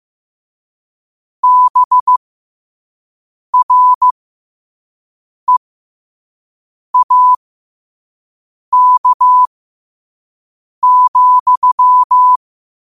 Morse code, heard as a keyed tone, about 15 wpm, spaced as for 5 wpm, 1000 Hz. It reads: BREAK,
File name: qotdfiles / 5wpm